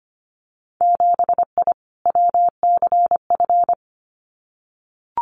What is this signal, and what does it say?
Morse code, heard as a keyed tone, about 25 wpm, 700 Hz. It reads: 7I WCF
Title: Winter Field Day Exchanges Encore 25wpm